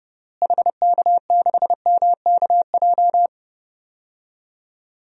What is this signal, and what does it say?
Morse code, heard as a keyed tone, about 31 wpm, 700 Hz. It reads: HK6MKJ
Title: DXCC-100 Call Signs - 6 Characters - Repeated 3x - 30wpm